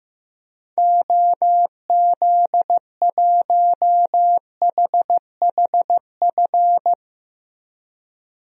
Morse code, heard as a keyed tone, about 15 wpm, 700 Hz. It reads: OZ1HHF
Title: DXCC-100 Call Signs - 6 Characters - 15wpm